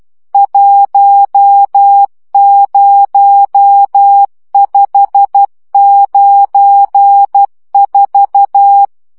Morse code, heard as a keyed tone, about 12 wpm, 800 Hz. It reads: 10594
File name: morze vizsga 2011-1 10